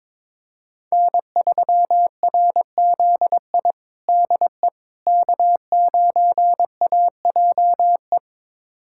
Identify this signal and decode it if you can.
Morse code, heard as a keyed tone, about 22 wpm, 700 Hz. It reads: N3RZI DE K9AJE
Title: USA Callsign DE Callsign - 5 Characters - 22wpm